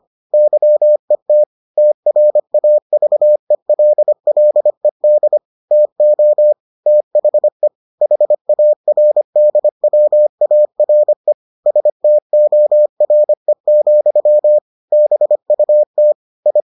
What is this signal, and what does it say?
Morse code, heard as a keyed tone, about 25 wpm, 600 Hz. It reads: YET TRAVELLED TO THE HARDWARE STORE, BUT I